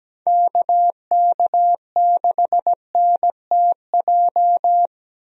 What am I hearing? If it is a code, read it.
Morse code, heard as a keyed tone, about 17 wpm, 700 Hz. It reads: KK6NTJ
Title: USA Call Signs - 6 Characters - Repeated 3x - 17wpm